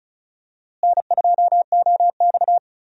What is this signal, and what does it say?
Morse code, heard as a keyed tone, about 36 wpm, 700 Hz. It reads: N2OX
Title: USA Call Signs - 4 Characters - Repeated 3x - 35wpm